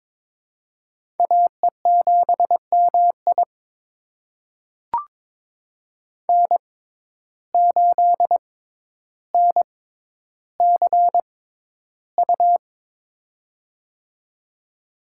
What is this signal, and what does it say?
Morse code, heard as a keyed tone, about 22 wpm, spaced as for 11 wpm, 700 Hz. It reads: AE7MI N8NCU
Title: USA Call Signs - 5 Characters - 6x Character Spacing - 22wpm